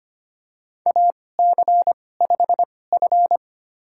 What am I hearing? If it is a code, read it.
Morse code, heard as a keyed tone, about 25 wpm, spaced as for 18 wpm, 700 Hz. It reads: AC5F
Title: USA Call Signs - 2x Character Spacing - 25wpm